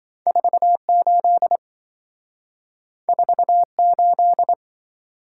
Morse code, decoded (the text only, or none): 48 48